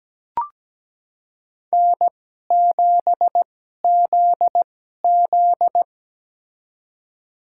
Morse code, heard as a keyed tone, about 17 wpm, spaced as for 12 wpm, 700 Hz. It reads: N7ZZ
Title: USA Call Signs - 4 Characters - 2x Character Spacing - 17wpm